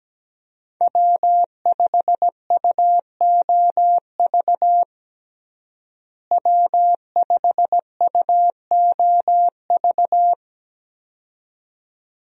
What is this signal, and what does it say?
Morse code, heard as a keyed tone, about 17 wpm, 700 Hz. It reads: W5UOV W5UOV
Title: USA Call Signs - 5 Characters - Repeated 2x - 17wpm